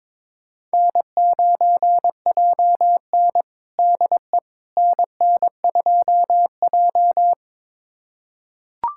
Morse code, heard as a keyed tone, about 21 wpm, 700 Hz. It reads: N9JN DE NN2J
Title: USA Callsign DE Callsign - 4 Characters - 22wpm